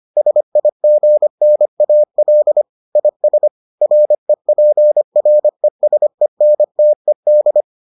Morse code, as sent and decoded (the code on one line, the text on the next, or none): ... .. --. -. .- .-.. / .. ... / .-. .--. .-. ... -. - . -..
SIGNAL IS REPRESENTED